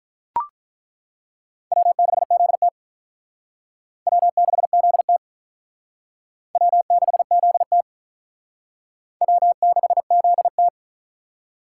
Morse code, heard as a keyed tone, about 42 wpm, 700 Hz. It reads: W6ZT W6ZT W6ZT W6ZT
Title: Speed Racing - USA 1x2 Call Signs 35wpm